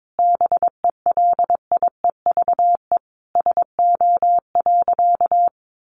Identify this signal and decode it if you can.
Morse code, heard as a keyed tone, about 22 wpm, 700 Hz. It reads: BELIEVE SO.
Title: Sentences from Top 1000 Words 22wpm